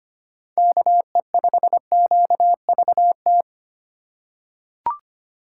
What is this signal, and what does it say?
Morse code, heard as a keyed tone, about 25 wpm, 700 Hz. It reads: KE5QVT
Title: USA Call Signs - 6 Characters - 4x Character Spacing - 25wpm